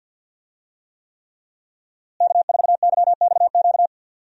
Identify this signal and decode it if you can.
Morse code, heard as a keyed tone, about 52 wpm, 700 Hz. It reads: K4YXQ
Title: USA Call Signs - 5 Characters - Repeated 3x - 50wpm